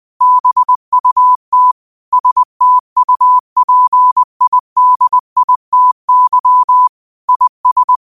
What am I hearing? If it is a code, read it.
Morse code, heard as a keyed tone, about 20 wpm, 1000 Hz. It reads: BUT STUPIDITY IS
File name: qotdfiles / 20wpm